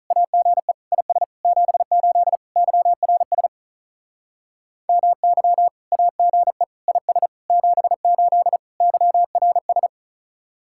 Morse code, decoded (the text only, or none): AGE IS 78 YRS MY AGE IS 78 YRS